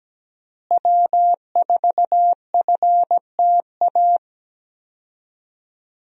Morse code, heard as a keyed tone, about 17 wpm, 700 Hz. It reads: W4FTA
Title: USA Call Signs - 5 Characters - Repeated 2x - 17wpm